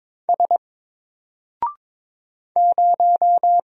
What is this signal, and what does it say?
Morse code, heard as a keyed tone, about 22 wpm, 700 Hz. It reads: S 0
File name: Single-Letter-Number-22wpm